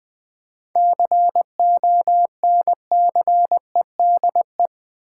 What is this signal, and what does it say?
Morse code, heard as a keyed tone, about 20 wpm, 700 Hz. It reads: CONCEDE
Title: 7 Letter Words - First 3 Letters Given - 20wpm